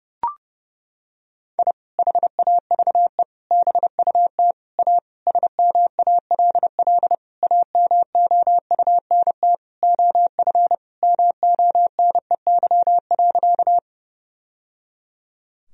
Morse code, heard as a keyed tone, about 30 wpm, 700 Hz. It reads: I HAVE BUT A SMALL AMOUNT OF MONEY.
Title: Sentences from Top 1000 Words - Review Part 4 - 30wpm